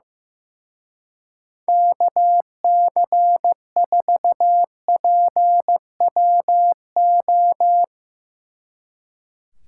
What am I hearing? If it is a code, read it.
Morse code, heard as a keyed tone, about 16 wpm, 700 Hz. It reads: KC4PWO
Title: Speed Racing - USA 2x3 Call Signs 15wpm